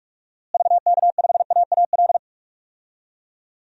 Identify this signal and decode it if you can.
Morse code, heard as a keyed tone, about 46 wpm, 700 Hz. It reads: VK5AAL
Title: DXCC-100 Call Signs - 6 Characters - Repeated 2x - 45wpm